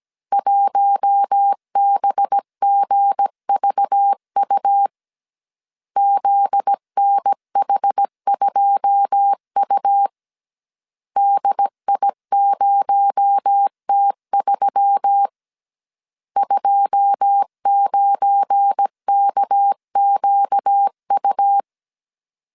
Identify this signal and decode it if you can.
Morse code, heard as a keyed tone, about 17 wpm, 800 Hz. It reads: 1BGVU ZNH2U DI0T3 29KQU
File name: CW / DET / 1 9 1500C S A 16wpm 2828